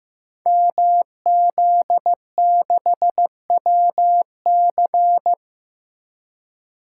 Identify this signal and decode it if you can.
Morse code, heard as a keyed tone, about 15 wpm, 700 Hz. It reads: MZ6WC
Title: Code Groups - 5 Letter:Numbers - 15wpm